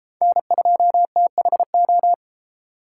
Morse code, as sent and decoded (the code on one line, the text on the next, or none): -. ..--- - .... ---
N2THO